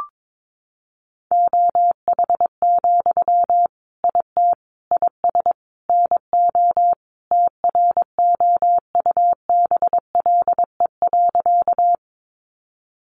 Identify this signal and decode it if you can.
Morse code, heard as a keyed tone, about 22 wpm, 700 Hz. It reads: OH, IT IS NO TROUBLE.
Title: Sentences from Top 1200 Words 22wpm